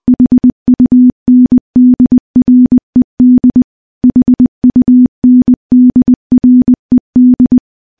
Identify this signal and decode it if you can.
Morse code, heard as a keyed tone, about 21 wpm, 250 Hz. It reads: HUNDRED HUNDRED
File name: morse-training-20WPM